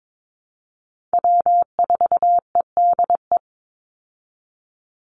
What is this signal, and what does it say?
Morse code, heard as a keyed tone, about 22 wpm, 700 Hz. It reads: W4EDE
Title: Call Signs Repeated 2x - 22wpm